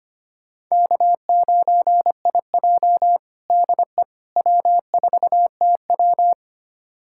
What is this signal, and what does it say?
Morse code, heard as a keyed tone, about 25 wpm, 700 Hz. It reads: K9IJ DE W4TW
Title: USA Callsign DE Callsign - 4 Characters - 25wpm